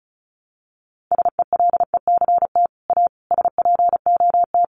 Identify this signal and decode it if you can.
Morse code, heard as a keyed tone, about 34 wpm, 700 Hz. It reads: SELECT A SPOT
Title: Sentences from Top 1300 Words 35wpm